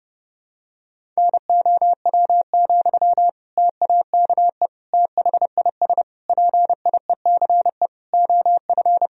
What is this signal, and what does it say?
Morse code, heard as a keyed tone, about 32 wpm, 700 Hz. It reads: NOW, TAKE THIS PIECE OF
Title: Sentences from Top 400 Words and Abbreviations 30wpm